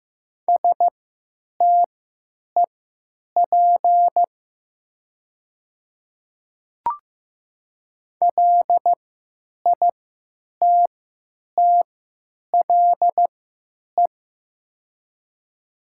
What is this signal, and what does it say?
Morse code, heard as a keyed tone, about 15 wpm, spaced as for 9 wpm, 700 Hz. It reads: STEP LITTLE
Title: 3x Character Spacing - Top 500 Words - 15wpm